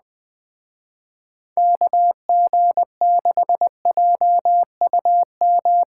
Morse code, heard as a keyed tone, about 21 wpm, 700 Hz. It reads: KG6JUM